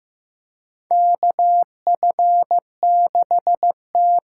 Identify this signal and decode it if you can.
Morse code, heard as a keyed tone, about 16 wpm, 700 Hz. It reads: KF6T
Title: K1USN SST QSO Practice - 15wpm